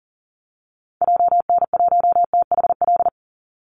Morse code, heard as a keyed tone, about 40 wpm, 700 Hz. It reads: JN1THL
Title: Call Signs - 40wpm